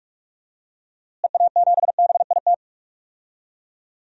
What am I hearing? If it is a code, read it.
Morse code, heard as a keyed tone, about 46 wpm, 700 Hz. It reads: EA7BIT
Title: DXCC-100 Call Signs - 6 Characters - Repeated 2x - 45wpm